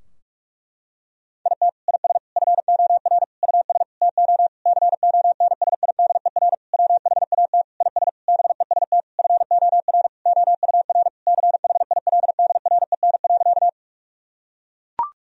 Morse code, heard as a keyed tone, about 41 wpm, 700 Hz. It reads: IT IS FOR US TO CONSIDER WHAT IS BEST FOR OUR CHILDREN.
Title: Sentences from Top 1000 Words - Review Part 4 - 45wpm